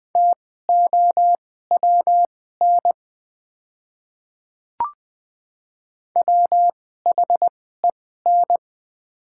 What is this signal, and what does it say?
Morse code, heard as a keyed tone, about 20 wpm, spaced as for 15 wpm, 700 Hz. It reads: TOWN WHEN
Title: 2x Character Spacing - Top 500 Words - 20wpm